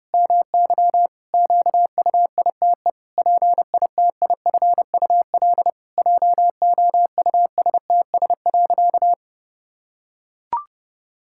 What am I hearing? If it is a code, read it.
Morse code, heard as a keyed tone, about 30 wpm, 700 Hz. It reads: MY QUITE PITIFUL JOUSTS.
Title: Pangram Challenge v2 - 30wpm